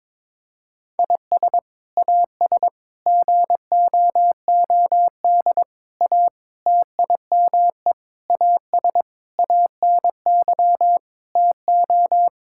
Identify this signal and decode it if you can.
Morse code, heard as a keyed tone, about 22 wpm, 700 Hz. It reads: IS AS GOOD A TIME AS ANY TO
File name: practice / 22wpm / Sets of 8 Words - 22wpm